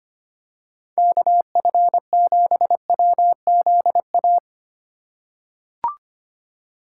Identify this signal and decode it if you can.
Morse code, heard as a keyed tone, about 25 wpm, 700 Hz. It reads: KF7WZA
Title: USA Call Signs - 4x Character Spacing - 25wpm